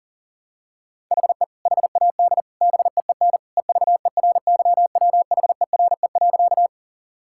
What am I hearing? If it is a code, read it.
Morse code, heard as a keyed tone, about 40 wpm, 700 Hz. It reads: HE HAD BEEN EVERYWHERE.
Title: Sentences from Top 1600 Words 40wpm